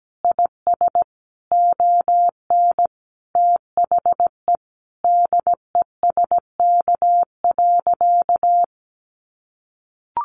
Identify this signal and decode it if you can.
Morse code, heard as a keyed tone, about 17 wpm, 700 Hz. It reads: IS ON THE DESK.
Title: Additional Easy Phrases 17wpm